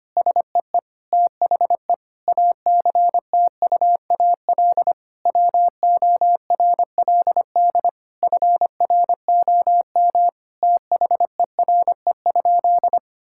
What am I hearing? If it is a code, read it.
Morse code, heard as a keyed tone, about 25 wpm, 700 Hz. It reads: SEE THE ACTUAL WORLD FROM THERE?
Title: Sentences from Top 1000 Words - Review Part 5 - 25wpm